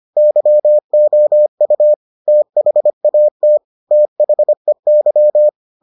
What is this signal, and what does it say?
Morse code, heard as a keyed tone, about 25 wpm, 600 Hz. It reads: YOU THAT THEY